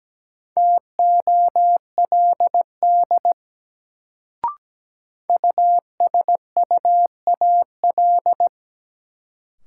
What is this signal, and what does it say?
Morse code, heard as a keyed tone, about 17 wpm, 700 Hz. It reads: TOLD USUAL
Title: Sentences from Top 400 Words - New Word Drill - 17wpm